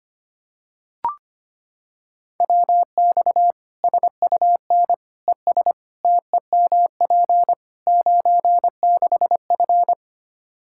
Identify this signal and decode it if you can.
Morse code, heard as a keyed tone, about 25 wpm, 700 Hz. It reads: WX SUN ES TEMP 96F
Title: Practice QSO - Exchange 2 - Info - 25wpm